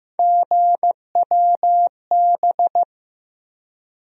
Morse code, heard as a keyed tone, about 15 wpm, 700 Hz. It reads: GWB
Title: USA Call Sign Postfixes - 3 Letters 15wpm